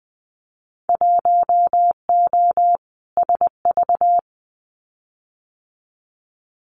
Morse code, heard as a keyed tone, about 20 wpm, 700 Hz. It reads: 1O SV
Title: Winter Field Day Exchanges 20wpm